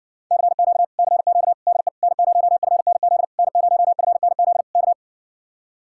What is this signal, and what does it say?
Morse code, heard as a keyed tone, about 60 wpm, 700 Hz. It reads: CQ CQ DE N0FNZ N0FNZ K